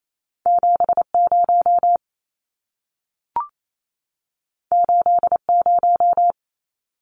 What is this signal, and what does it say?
Morse code, heard as a keyed tone, about 28 wpm, 700 Hz. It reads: 70 80
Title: Numbers - 2-Digit 28wpm